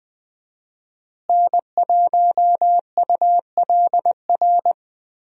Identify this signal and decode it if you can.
Morse code, heard as a keyed tone, about 21 wpm, 700 Hz. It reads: N1ULR
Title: USA Call Signs - 5 Characters - Repeated 2x - 20wpm